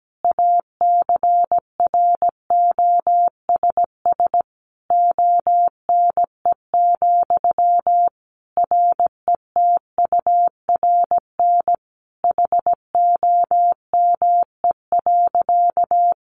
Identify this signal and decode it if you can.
Morse code, heard as a keyed tone, about 17 wpm, 700 Hz. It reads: ACROSS ONE, RETURN HOME.
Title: Sentences from Top 800 Words 17wpm